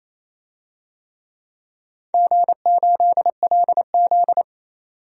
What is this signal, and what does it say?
Morse code, heard as a keyed tone, about 28 wpm, 700 Hz. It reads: G8LZ
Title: DXCC-100 Call Signs - 4 Characters - 28wpm